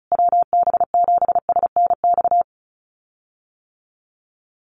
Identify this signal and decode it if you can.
Morse code, heard as a keyed tone, about 34 wpm, 700 Hz. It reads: WB7SNX
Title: USA Call Signs Repeated 3x 35wpm